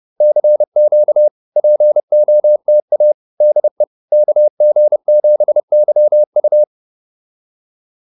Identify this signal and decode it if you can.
Morse code, heard as a keyed tone, about 30 wpm, 590 Hz. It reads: CQ POTA DE KG7YU